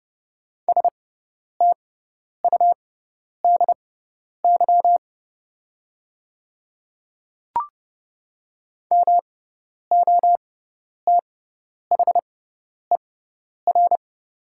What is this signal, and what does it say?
Morse code, heard as a keyed tone, about 30 wpm, spaced as for 10 wpm, 700 Hz. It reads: STUDY MOTHER
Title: 6x Character Spacing - Top 300 Words - 30wpm